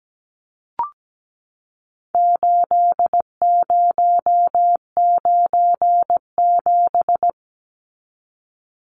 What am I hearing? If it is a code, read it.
Morse code, heard as a keyed tone, about 17 wpm, 700 Hz. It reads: 8097
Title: Numbers - 4-Digit 17wpm